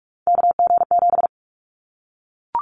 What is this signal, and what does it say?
Morse code, heard as a keyed tone, about 44 wpm, 700 Hz. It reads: KG7
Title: Callsign Prefixes for Top 25 DXCC - 45wpm